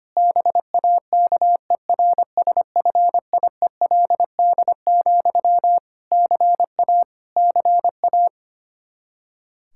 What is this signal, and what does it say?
Morse code, heard as a keyed tone, about 25 wpm, 700 Hz. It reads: BAKERSFIELD, CA CA
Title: Practice QSO - Exchange 1 - Beginning - 25wpm